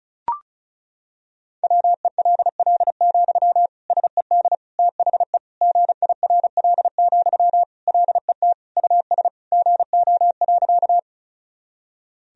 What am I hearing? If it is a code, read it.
Morse code, heard as a keyed tone, about 36 wpm, 700 Hz. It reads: WELL, SED THE GIRL, LET US GO.
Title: Sentences from Top 400 Words and Abbreviations 35wpm